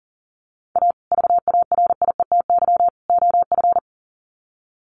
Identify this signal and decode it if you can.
Morse code, heard as a keyed tone, about 40 wpm, 700 Hz. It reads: A VARIETY OF
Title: Sets of 3 Words 40wpm